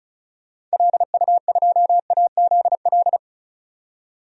Morse code, heard as a keyed tone, about 36 wpm, 690 Hz. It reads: LU2AZL
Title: DXCC-100 Call Signs - 6 Characters - 35wpm